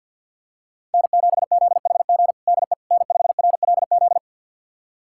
Morse code, heard as a keyed tone, about 52 wpm, 700 Hz. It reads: N7ZHG DE N5RLZ
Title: USA Callsign DE Callsign - 5 Characters - 50wpm